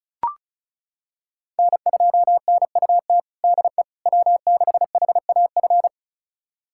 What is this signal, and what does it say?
Morse code, heard as a keyed tone, about 36 wpm, 700 Hz. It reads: N2NUT DE W6HAF
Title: USA Callsign DE Callsign - 5 Characters - 35wpm